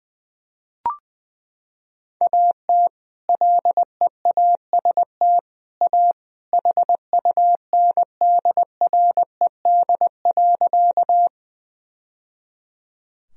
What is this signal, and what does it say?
Morse code, heard as a keyed tone, about 20 wpm, 700 Hz. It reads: AT LEAST A HUNDRED.
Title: Sentences from Top 1000 Words - Review Part 2 - 20wpm